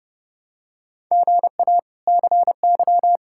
Morse code, heard as a keyed tone, about 30 wpm, 700 Hz. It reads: GA CY
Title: K1USN SST QSO Practice - 30wpm